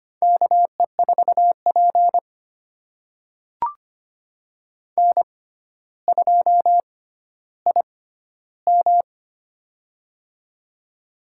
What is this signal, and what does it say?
Morse code, heard as a keyed tone, about 25 wpm, spaced as for 13 wpm, 700 Hz. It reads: KE4P N2IM
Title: USA Call Signs - 4 Characters - 6x Character Spacing - 25wpm